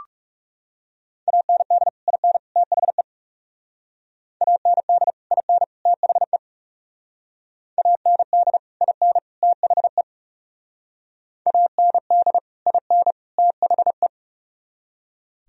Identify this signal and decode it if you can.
Morse code, heard as a keyed tone, about 37 wpm, 700 Hz. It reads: AND IN THE AND IN THE AND IN THE AND IN THE
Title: Speed-Racing - Sets of 3 Words 30wpm